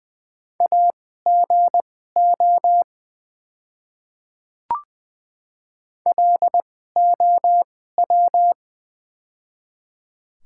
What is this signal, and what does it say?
Morse code, heard as a keyed tone, about 20 wpm, spaced as for 14 wpm, 700 Hz. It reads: AGO LOW
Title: 2x Character Spacing - Top 500 Words - 20wpm